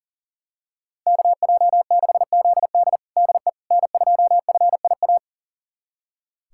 Morse code, heard as a keyed tone, about 40 wpm, 700 Hz. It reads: KJ6ZD DE N2FIA
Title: USA Callsign DE Callsign - 5 Characters - 40wpm